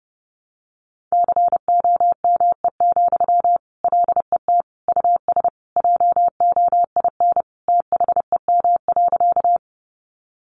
Morse code, heard as a keyed tone, about 30 wpm, 700 Hz. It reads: COME, LET US JOIN THEM.
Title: Sentences from Top 1400 Words 30wpm